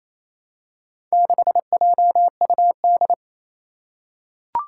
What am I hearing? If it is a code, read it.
Morse code, heard as a keyed tone, about 28 wpm, 700 Hz. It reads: 6JUD
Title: Four Challenging Letter-Number Combinations 28wpm